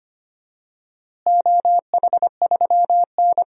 Morse code, heard as a keyed tone, about 25 wpm, 700 Hz. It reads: OH3N
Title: DXCC-100 Call Signs - 4 Characters - Repeated 3x - 25wpm